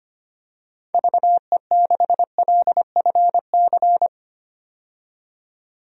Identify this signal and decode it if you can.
Morse code, heard as a keyed tone, about 25 wpm, 700 Hz. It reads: VE6LFC
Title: DXCC-100 Call Signs - 6 Characters - 25wpm